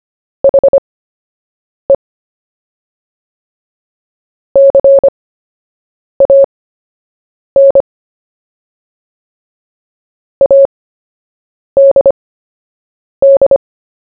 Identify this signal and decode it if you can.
Morse code, heard as a keyed tone, about 25 wpm, spaced as for 7 wpm, 550 Hz. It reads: HE CAN ADD
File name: cw / fists / Short-Story-1-7wpm